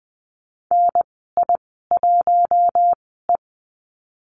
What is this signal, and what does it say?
Morse code, heard as a keyed tone, about 20 wpm, spaced as for 15 wpm, 700 Hz. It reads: NI1E
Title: 076 Lesson 13g - Letter L - Call Signs 20wpm